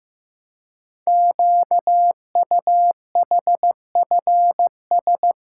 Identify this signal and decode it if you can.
Morse code, heard as a keyed tone, about 15 wpm, 700 Hz. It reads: QUHFS
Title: Code Groups - 5 Letters - 15wpm